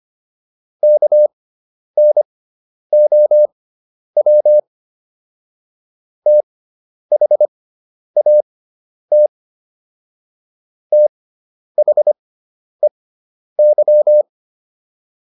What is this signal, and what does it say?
Morse code, heard as a keyed tone, about 25 wpm, spaced as for 10 wpm, 600 Hz. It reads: KNOW THAT THEY